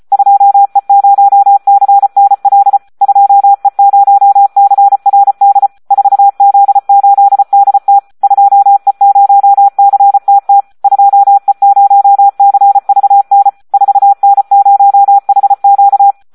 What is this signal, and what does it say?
Morse code, heard as a keyed tone, about 34 wpm, 800 Hz, with radio noise. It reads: 2E0CNL 2E0CRD 4Z8DT 2E0CTT 2E0CVN 4N0HQ
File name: Audio practica CW / SD 1 34